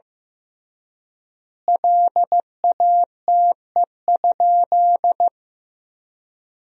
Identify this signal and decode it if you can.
Morse code, heard as a keyed tone, about 16 wpm, 700 Hz. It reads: LATE?
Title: 3x Word-Spacing - Sentences from Top 300 Words - 15wpm